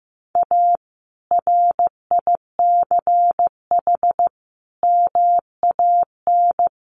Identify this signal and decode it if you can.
Morse code, heard as a keyed tone, about 15 wpm, 700 Hz. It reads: A RICH MAN
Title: Sentences from Top 1100 Words 15wpm